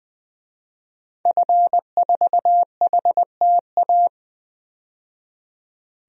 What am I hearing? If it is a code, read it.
Morse code, heard as a keyed tone, about 20 wpm, 700 Hz. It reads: F4HTA
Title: DXCC-100 Call Signs - 5 Characters - 20wpm